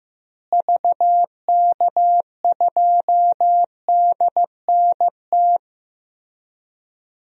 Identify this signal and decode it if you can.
Morse code, heard as a keyed tone, about 15 wpm, 700 Hz. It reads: VK2DNT
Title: DXCC-100 Call Signs - 6 Characters - Repeated 3x - 15wpm